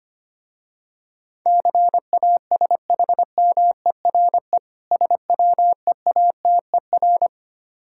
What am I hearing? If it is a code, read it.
Morse code, heard as a keyed tone, about 25 wpm, 700 Hz. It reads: CASHMERE SWEATER
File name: practice / 25wpm / Difficult Words with Hints - 25wpm